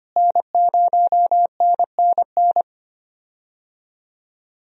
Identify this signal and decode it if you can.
Morse code, heard as a keyed tone, about 25 wpm, 700 Hz. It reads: N0NNN
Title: USA Call Signs - 5 Characters - Repeated 3x - 25wpm